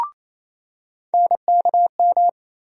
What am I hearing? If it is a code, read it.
Morse code, heard as a keyed tone, about 28 wpm, 700 Hz. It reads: NKM
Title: USA Call Sign Postfixes - 3 Letters 28wpm